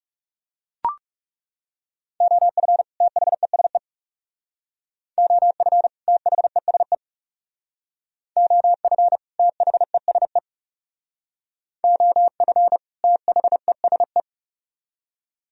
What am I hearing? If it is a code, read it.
Morse code, heard as a keyed tone, about 37 wpm, 700 Hz. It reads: OF THESE OF THESE OF THESE OF THESE
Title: Speed-Racing - Sets of 2 Words 30wpm